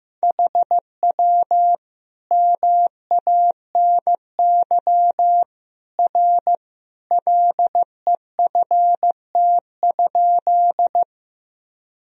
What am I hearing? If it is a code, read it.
Morse code, heard as a keyed tone, about 15 wpm, 700 Hz. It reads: HW MANY R LEFT?
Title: Sentences from Top 300 Words and Abbreviations 15wpm